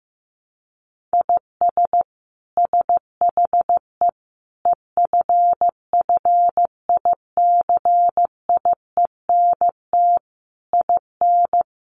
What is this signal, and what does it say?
Morse code, heard as a keyed tone, about 15 wpm, 700 Hz. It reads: IS SHE EFFICIENT IN
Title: Sentences from Top 1300 Words 15wpm